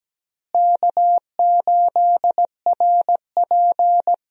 Morse code, heard as a keyed tone, about 17 wpm, 700 Hz. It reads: K8RP
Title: USA Call Signs - 4 Characters - Repeated 3x - 17wpm